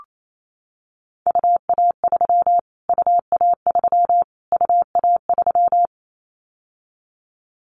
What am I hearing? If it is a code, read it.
Morse code, heard as a keyed tone, about 28 wpm, 700 Hz. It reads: UA3 UA3 UA3
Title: Callsign Prefixes for Top 25 DXCC - Repeated 3x - 28wpm